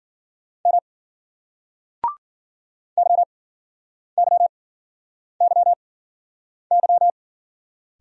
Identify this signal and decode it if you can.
Morse code, heard as a keyed tone, about 46 wpm, spaced as for 9 wpm, 700 Hz. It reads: M YYYY
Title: Speed-Racing Single Letters 40wpm